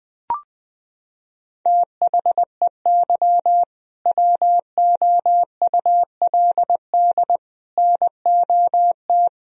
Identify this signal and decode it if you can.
Morse code, heard as a keyed tone, about 19 wpm, 700 Hz. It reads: THEY WOULD NOT
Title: Sentences from Top 1000 Words - Review Part 3 - 20wpm